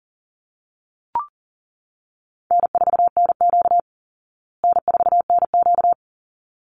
Morse code, heard as a keyed tone, about 40 wpm, 700 Hz. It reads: N4NQ N4NQ
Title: Easy Call Signs Repeated 2x - 40wpm